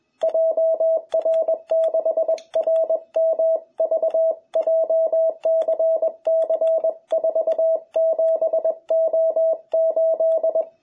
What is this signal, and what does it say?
Morse code, heard as a keyed tone, about 20 wpm, 660 Hz, with radio noise. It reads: JR6RMV JCC47O8
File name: jh8cbh / 2021cw